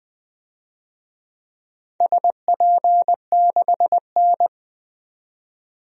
Morse code, heard as a keyed tone, about 21 wpm, 700 Hz. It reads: SP6N
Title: DXCC-100 Call Signs - 4 Characters - 20wpm